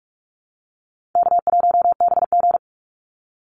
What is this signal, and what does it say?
Morse code, heard as a keyed tone, about 44 wpm, 700 Hz. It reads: K1BG
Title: Call Signs - 45wpm